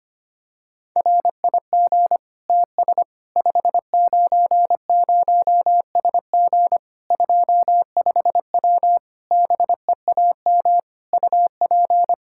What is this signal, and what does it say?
Morse code, heard as a keyed tone, about 25 wpm, 700 Hz. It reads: RIG TS 590SG 25W BEAM UP
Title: Practice QSO - Exchange 2 - Info - 25wpm